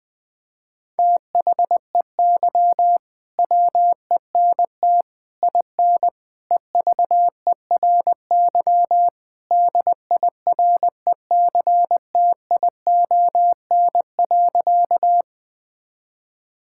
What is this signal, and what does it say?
Morse code, heard as a keyed tone, about 20 wpm, 700 Hz. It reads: THEY WENT IN EVERY DIRECTION.
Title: Sentences from Top 1000 Words - Review Part 5 - 20wpm